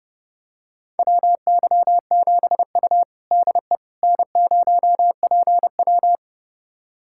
Morse code, heard as a keyed tone, about 30 wpm, 700 Hz. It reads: WY7U DE N0PW
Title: USA Callsign DE Callsign - 4 Characters - 30wpm